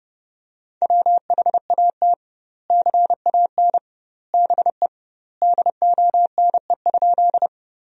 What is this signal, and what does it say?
Morse code, heard as a keyed tone, about 30 wpm, 700 Hz. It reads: WHAT CAN BE DONE?
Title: 2x Word-Spacing - Sentences from Top 500 Words - 30wpm